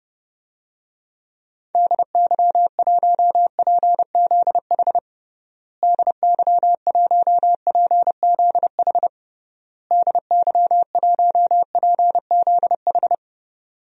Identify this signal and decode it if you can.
Morse code, heard as a keyed tone, about 32 wpm, 700 Hz. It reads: DY1PZH DY1PZH DY1PZH
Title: DXCC-100 Call Signs - 6 Characters - Repeated 3x - 30wpm